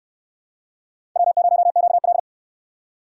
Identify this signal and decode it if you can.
Morse code, heard as a keyed tone, about 64 wpm, 700 Hz. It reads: W0QL